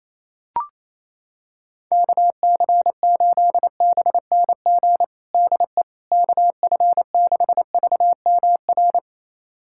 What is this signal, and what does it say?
Morse code, heard as a keyed tone, about 28 wpm, 700 Hz. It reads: KC8BNG DE KF6VMR
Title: USA Callsign DE Callsign - 6 Characters - 28wpm